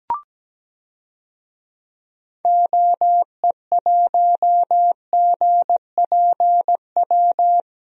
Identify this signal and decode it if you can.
Morse code, heard as a keyed tone, about 17 wpm, 700 Hz. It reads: OE1GPW
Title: DXCC-100 Call Signs - 6 Characters - Repeated 3x - 17wpm